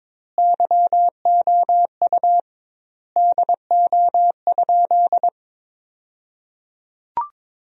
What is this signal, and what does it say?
Morse code, heard as a keyed tone, about 22 wpm, 700 Hz. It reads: YOU DO?
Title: 2x Word-Spacing - Sentences from Top 100 Words - 22wpm